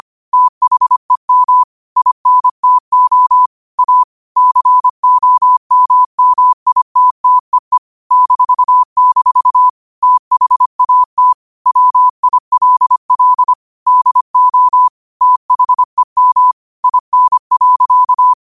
Morse code, heard as a keyed tone, about 25 wpm, 1000 Hz. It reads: THEM INTO A COMMITTEE -- THAT WILL DO THEM IN.
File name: qotdfiles / 25wpm